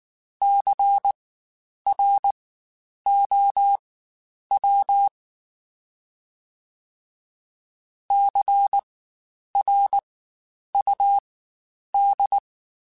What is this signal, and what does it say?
Morse code, heard as a keyed tone, about 19 wpm, spaced as for 9 wpm, 800 Hz. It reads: CROW CRUD
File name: Spaced Alphas / Spaced Alphas mp3s / Cs sp @18